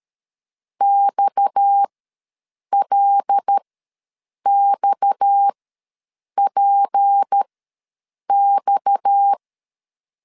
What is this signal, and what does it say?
Morse code, heard as a keyed tone, about 13 wpm, spaced as for 7 wpm, 800 Hz. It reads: XLXPX
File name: LEZIONI / 6 PKX0 / 6 PKX ws 12wpm 3106